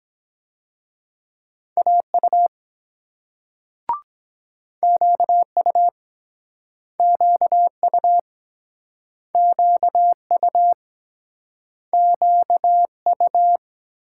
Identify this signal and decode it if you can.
Morse code, heard as a keyed tone, about 24 wpm, 700 Hz. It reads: AU QU QU QU QU